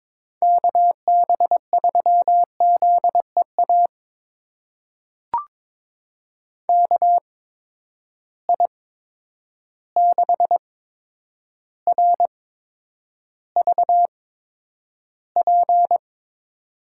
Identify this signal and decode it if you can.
Morse code, heard as a keyed tone, about 21 wpm, spaced as for 9 wpm, 700 Hz. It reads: KB3ZEA KI6RVP
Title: USA Call Signs - 6 Characters - 8x Character Spacing - 22wpm